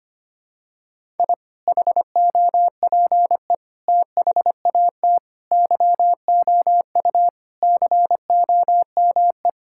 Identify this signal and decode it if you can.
Morse code, heard as a keyed tone, about 25 wpm, 700 Hz. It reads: I HOPE THAT YOU COME
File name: practice / 25wpm / Sentences with 2-Word Phrases - 25wpm